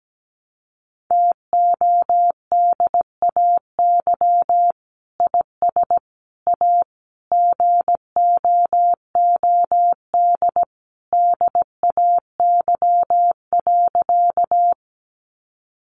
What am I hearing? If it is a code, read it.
Morse code, heard as a keyed tone, about 17 wpm, 700 Hz. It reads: TODAY IS A GOOD DAY.
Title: Sentences from Top 600 Words 17wpm